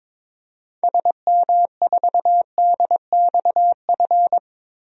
Morse code, heard as a keyed tone, about 22 wpm, 700 Hz. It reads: SM4DXF
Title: DXCC-100 Call Signs - 6 Characters - Repeated 2x - 22wpm